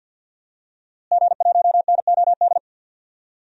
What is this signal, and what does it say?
Morse code, heard as a keyed tone, about 51 wpm, 700 Hz. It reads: G1NOD
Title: DXCC-100 Call Signs - 5 Characters - Repeated 2x - 50wpm